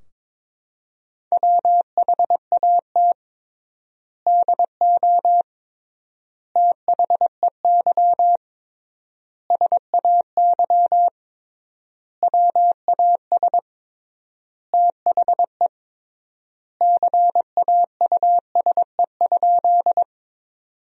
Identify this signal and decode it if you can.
Morse code, heard as a keyed tone, about 22 wpm, 700 Hz. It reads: WHAT DO THEY SAY WAS THE CAUSE?
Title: 3x Word-Spacing - Sentences from Top 200 Words - 22wpm